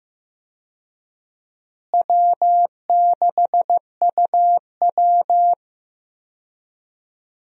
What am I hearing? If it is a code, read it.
Morse code, heard as a keyed tone, about 15 wpm, 700 Hz. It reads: W6UW
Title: USA Call Signs - 4 Characters - Repeated 2x - 15wpm